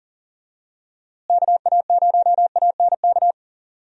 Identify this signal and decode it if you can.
Morse code, heard as a keyed tone, about 40 wpm, 700 Hz. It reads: KA0ANK
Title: USA Call Signs - 6 Characters - Repeated 3x - 40wpm